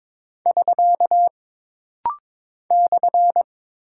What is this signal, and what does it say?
Morse code, heard as a keyed tone, about 22 wpm, 700 Hz. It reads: <SK> /
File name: practice / 22wpm / Single Punctuation-Prosign - Rapid-Fire - 22wpm